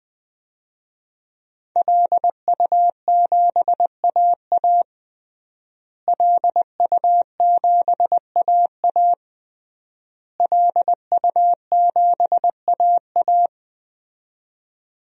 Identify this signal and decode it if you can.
Morse code, heard as a keyed tone, about 21 wpm, 700 Hz. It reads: LU7AA LU7AA LU7AA
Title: DXCC-100 Call Signs - 5 Characters - Repeated 3x - 20wpm